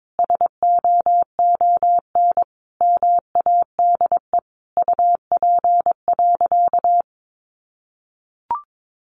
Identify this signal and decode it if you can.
Morse code, heard as a keyed tone, about 22 wpm, 700 Hz. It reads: SOON MADE UP.
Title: Sentences from Top 500 Words 22wpm